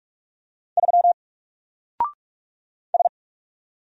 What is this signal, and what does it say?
Morse code, heard as a keyed tone, about 44 wpm, 700 Hz. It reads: 3 S
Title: Single Character (Letter:Number:Punctuation:Prosign) Alt 45wpm